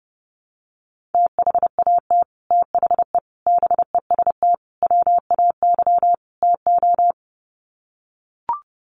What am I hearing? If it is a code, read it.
Morse code, heard as a keyed tone, about 30 wpm, 700 Hz. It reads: THAT THE BEST WAY TO
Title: Sets of 5 Words Encore2 30wpm